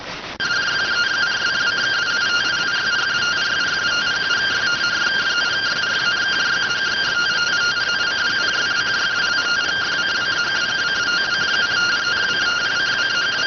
Two-tone teleprinter signal, about 170 Hz shift.